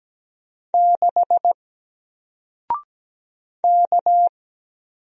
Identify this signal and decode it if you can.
Morse code, heard as a keyed tone, about 17 wpm, 700 Hz. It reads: 6 K
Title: Single Letter-Number 17wpm